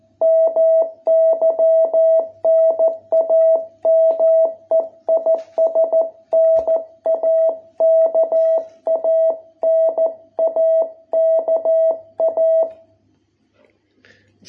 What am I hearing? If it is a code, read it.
Morse code, heard as a keyed tone, about 14 wpm, 660 Hz. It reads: MY NAME IS NAKA NAKA